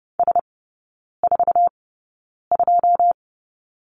Morse code, heard as a keyed tone, about 30 wpm, spaced as for 9 wpm, 700 Hz. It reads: S42
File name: 081 Lesson 14e - Number 2 - Three Characters 30wpm